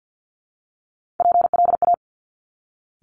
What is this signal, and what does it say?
Morse code, heard as a keyed tone, about 50 wpm, 700 Hz. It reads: PLA
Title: Top Three Letter Combinations in Words 50wpm